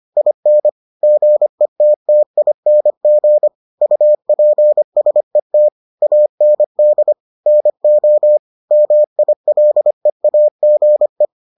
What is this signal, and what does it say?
Morse code, heard as a keyed tone, about 25 wpm, 600 Hz. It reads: IN GETTING UPSET AND NO MILEAGE